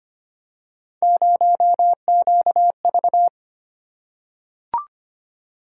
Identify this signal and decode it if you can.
Morse code, heard as a keyed tone, about 25 wpm, 700 Hz. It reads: 0QV
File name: Three Letter-Number Combinations - 25wpm